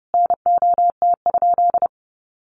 Morse code, heard as a keyed tone, about 30 wpm, 700 Hz. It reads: NOT?
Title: Sentences from Top 1300 Words 30wpm